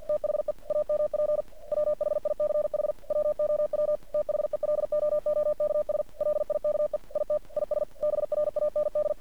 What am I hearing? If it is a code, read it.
Morse code, heard as a keyed tone, about 50 wpm, 600 Hz. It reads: THE AMP WHICH WOW THELOOKS LIKE IT IS BRAND
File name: archived / CRSnet